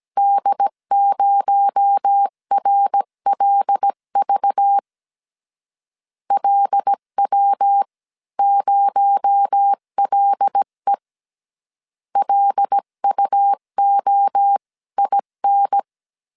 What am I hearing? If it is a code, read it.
Morse code, heard as a keyed tone, about 17 wpm, 800 Hz. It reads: D0RLV LW0LE LUOIN
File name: CW / DET / 1 6 1500C S C 16wpm 2733